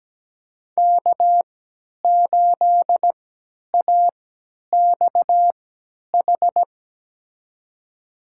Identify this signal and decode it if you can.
Morse code, heard as a keyed tone, about 17 wpm, spaced as for 10 wpm, 700 Hz. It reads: K8AXH
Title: USA Call Signs - 5 Characters - 3x Character Spacing - 17wpm